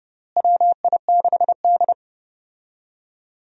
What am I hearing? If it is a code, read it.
Morse code, heard as a keyed tone, about 31 wpm, 700 Hz. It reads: WI6D
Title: USA Call Signs - 4 Characters - Repeated 3x - 30wpm